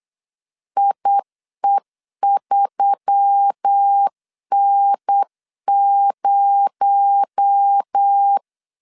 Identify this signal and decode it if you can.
Morse code, heard as a keyed tone, about 8 wpm, 800 Hz. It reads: IE3N0
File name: CW / DET / 1 6 1500C S C 8wpm 5219